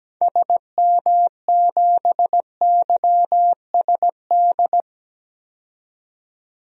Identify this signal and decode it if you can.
Morse code, heard as a keyed tone, about 17 wpm, 700 Hz. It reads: SM7YSD
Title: DXCC-100 Call Signs - 6 Characters - 17wpm